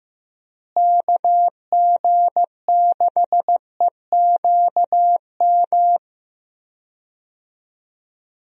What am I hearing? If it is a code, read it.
Morse code, heard as a keyed tone, about 16 wpm, 700 Hz. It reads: KG6EQM
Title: USA Call Signs - 6 Characters - Repeated 3x - 15wpm